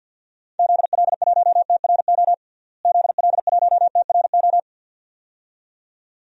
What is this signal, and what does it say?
Morse code, heard as a keyed tone, about 53 wpm, 700 Hz. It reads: ZL1TRO ZL1TRO
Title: DXCC-100 Call Signs - 6 Characters - Repeated 2x - 50wpm